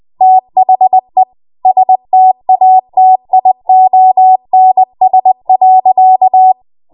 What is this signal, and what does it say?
Morse code, heard as a keyed tone, about 20 wpm, 750 Hz. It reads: THE STATIONS.
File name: Morse / 20WPM